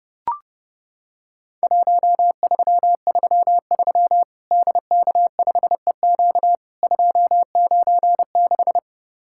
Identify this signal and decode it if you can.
Morse code, heard as a keyed tone, about 30 wpm, 700 Hz. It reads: 1333 DK5EQ 296
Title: Worked All Europe DX Contest - QTC Exchange - 30wpm